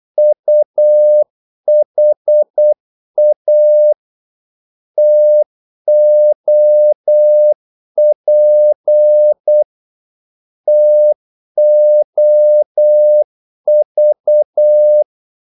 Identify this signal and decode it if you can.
Morse code, heard as a keyed tone, about 8 wpm, 600 Hz. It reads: UHA TOP TOV
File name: Korte ord DK 8wpm